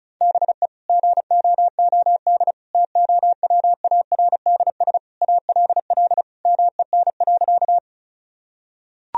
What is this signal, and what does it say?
Morse code, heard as a keyed tone, about 36 wpm, 700 Hz. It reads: BE GOOD TOWARDS ALL MEN.
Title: Sentences from Top 1000 Words - Review Part 2 - 35wpm